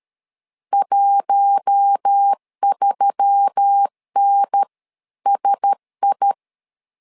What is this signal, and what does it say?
Morse code, heard as a keyed tone, about 13 wpm, 800 Hz. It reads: 13NSI
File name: LEZIONI / 5 LBVC / 5 LBVC 12wpm 1955